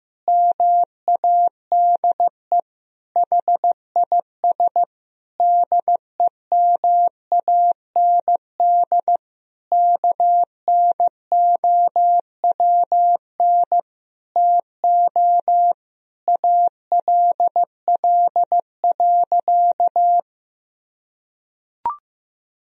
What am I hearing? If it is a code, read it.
Morse code, heard as a keyed tone, about 15 wpm, 700 Hz. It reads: MADE HIS DEMAND KNOWN TO ALL.
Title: Sentences from Top 1000 Words - Review Part 4 - 15wpm